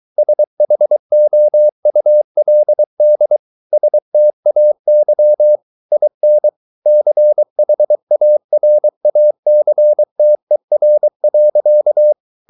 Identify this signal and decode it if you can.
Morse code, heard as a keyed tone, about 23 wpm, 600 Hz. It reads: SHOULD STAY IN CHARACTER.